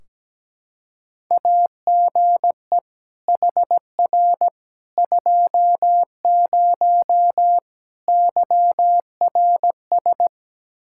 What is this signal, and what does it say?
Morse code, heard as a keyed tone, about 17 wpm, 700 Hz. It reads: AGE HR 20 YRS
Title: K1USN SST QSO Practice 17wpm